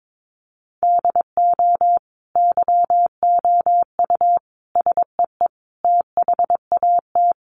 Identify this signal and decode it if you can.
Morse code, heard as a keyed tone, about 22 wpm, 700 Hz. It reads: DO YOU SEE THAT
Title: Sentences from Top 500 Words 22wpm